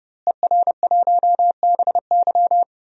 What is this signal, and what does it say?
Morse code, heard as a keyed tone, about 30 wpm, 700 Hz. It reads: ER1BY
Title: DXCC-100 Call Signs - 5 Characters - Repeated 2x - 30wpm